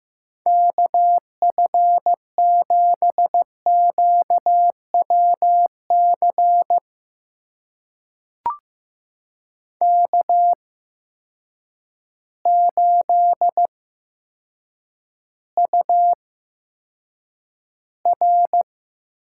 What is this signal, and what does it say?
Morse code, heard as a keyed tone, about 15 wpm, spaced as for 7 wpm, 700 Hz. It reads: KF7QWC K8UR
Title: USA Call Signs - 8x Character Spacing - 15wpm